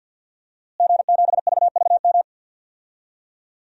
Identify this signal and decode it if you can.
Morse code, heard as a keyed tone, about 51 wpm, 700 Hz. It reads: G7VVM
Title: DXCC-100 Call Signs - 5 Characters - Repeated 3x - 50wpm